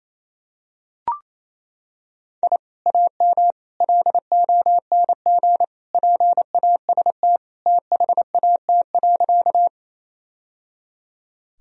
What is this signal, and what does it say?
Morse code, heard as a keyed tone, about 28 wpm, 700 Hz. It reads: I AM LONG PAST THAT.
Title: Sentences from Top 1000 Words - Review Part 3 - 28wpm